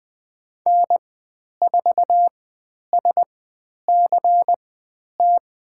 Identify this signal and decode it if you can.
Morse code, heard as a keyed tone, about 20 wpm, spaced as for 10 wpm, 700 Hz. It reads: N4SCT
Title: 095 Lesson 16g - Letter c - Call Signs 20wpm